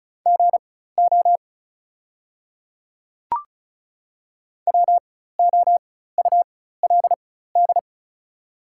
Morse code, heard as a keyed tone, about 34 wpm, spaced as for 16 wpm, 700 Hz. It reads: GO WOULD
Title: 4x Character Spacing - Top 100 Words - 35wpm